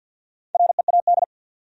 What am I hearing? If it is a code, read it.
Morse code, heard as a keyed tone, about 69 wpm, spaced as for 59 wpm, 700 Hz. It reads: READ
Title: Speed-Racing Top 200 Words 50wpm